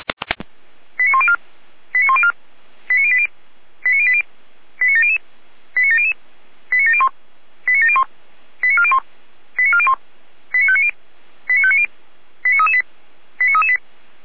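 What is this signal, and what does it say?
Five-tone selective call (ZVEI), heard as quick runs of five tones, tones of about 70 ms. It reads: 89104 89104 89080 89080 89700 89700 89701 89701 89401 89401 89590 89590 89338 89338
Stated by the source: ZVEI five-tone call received by SDR.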